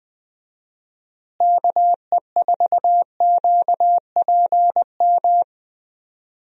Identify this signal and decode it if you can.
Morse code, heard as a keyed tone, about 21 wpm, 700 Hz. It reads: KE4QPM
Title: USA Call Signs - 6 Characters - Repeated 3x - 20wpm